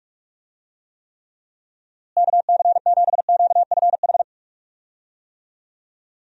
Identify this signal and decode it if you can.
Morse code, heard as a keyed tone, about 46 wpm, 700 Hz. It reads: KC7QFH
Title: USA Call Signs - 6 Characters - Repeated 3x - 45wpm